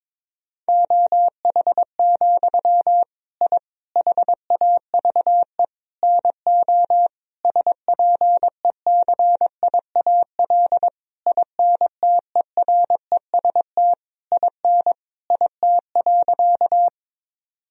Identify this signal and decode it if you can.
Morse code, heard as a keyed tone, about 22 wpm, 700 Hz. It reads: OH, I HAVE NO SPECIAL INTEREST IN IT.
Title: Sentences from Top 1000 Words - Review Part 3 - 22wpm